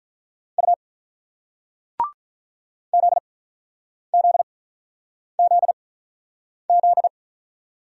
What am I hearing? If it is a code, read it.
Morse code, heard as a keyed tone, about 47 wpm, spaced as for 9 wpm, 700 Hz. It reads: U ZZZZ